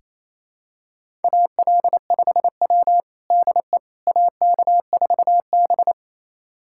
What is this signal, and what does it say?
Morse code, heard as a keyed tone, about 28 wpm, 700 Hz. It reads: AL5W DE AK4B
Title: USA Callsign DE Callsign - 4 Characters - 28wpm